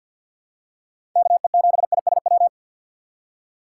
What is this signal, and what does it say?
Morse code, heard as a keyed tone, about 52 wpm, 700 Hz. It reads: KE7ISW
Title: USA Call Signs - 6 Characters - 3x Character Spacing - 50wpm